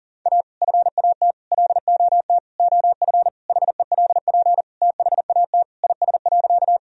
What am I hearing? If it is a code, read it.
Morse code, heard as a keyed tone, about 40 wpm, 700 Hz. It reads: A FAT LOT OF HELP THAT IS.
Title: Sentences from Top 1000 Words - Review Part 4 - 40wpm